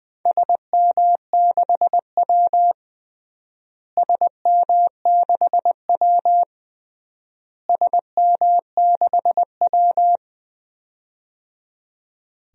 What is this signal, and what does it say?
Morse code, heard as a keyed tone, about 20 wpm, 700 Hz. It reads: SM6W SM6W SM6W
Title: DXCC-100 Call Signs - 4 Characters - Repeated 3x - 20wpm